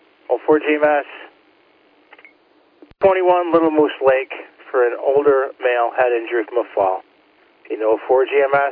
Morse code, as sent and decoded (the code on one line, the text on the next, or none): none